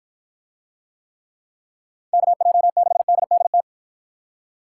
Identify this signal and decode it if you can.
Morse code, heard as a keyed tone, about 48 wpm, 700 Hz. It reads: KJ6DDT